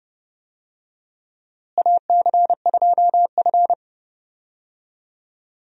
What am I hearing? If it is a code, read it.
Morse code, heard as a keyed tone, about 29 wpm, 700 Hz. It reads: AC2F
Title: USA Call Signs - 4 Characters - Repeated 3x - 30wpm